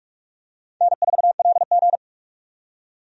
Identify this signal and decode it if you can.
Morse code, heard as a keyed tone, about 44 wpm, 700 Hz. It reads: N4LG
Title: USA Call Signs - 4 Characters - 45wpm